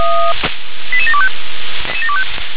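Five-tone selective call (ZVEI), heard as quick runs of five tones, tones of about 70 ms. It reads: 88725
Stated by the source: Five-tone ZVEI selective call 88702 received by SDR.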